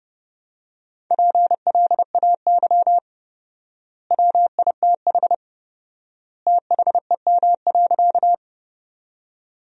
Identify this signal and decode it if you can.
Morse code, heard as a keyed tone, about 32 wpm, 700 Hz. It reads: PLAY WITH THEM.
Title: 4x Word-Spacing - Sentences from Top 200 Words - 30wpm